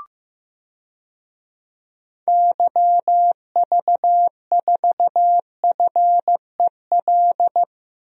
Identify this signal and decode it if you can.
Morse code, heard as a keyed tone, about 15 wpm, 700 Hz. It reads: YV4FEL
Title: DXCC-100 Call Signs - 6 Characters - Repeated 3x - 15wpm